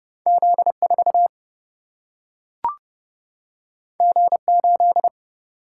Morse code, heard as a keyed tone, about 31 wpm, 700 Hz. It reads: Z4 G8
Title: Single-Letter Single-Number 30wpm